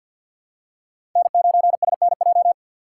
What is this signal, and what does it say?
Morse code, heard as a keyed tone, about 52 wpm, 700 Hz. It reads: N9SNJ
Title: USA Call Signs - 5 Characters - 2x Character Spacing - 50wpm